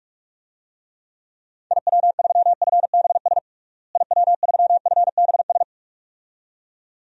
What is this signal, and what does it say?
Morse code, heard as a keyed tone, about 43 wpm, 700 Hz. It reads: IW3FBS IW3FBS
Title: DXCC-100 Call Signs - 6 Characters - Repeated 2x - 45wpm